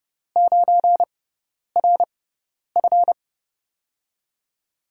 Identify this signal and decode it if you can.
Morse code, heard as a keyed tone, about 30 wpm, spaced as for 10 wpm, 700 Hz. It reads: 9RF
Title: USA Call Signs - 6x Character Spacing - 30wpm